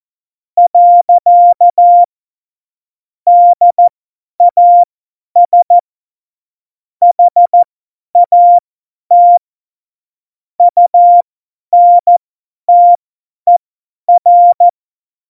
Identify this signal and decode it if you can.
Morse code, heard as a keyed tone, about 14 wpm, spaced as for 10 wpm, 700 Hz. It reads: . DAS HAT UNTER
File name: Morsekurs / Witze / Witz 05 Mai 2020, 50 F70